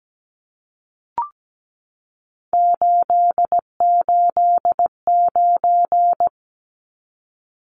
Morse code, heard as a keyed tone, about 17 wpm, 700 Hz. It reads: 889
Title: Numbers - 3-Digit 17wpm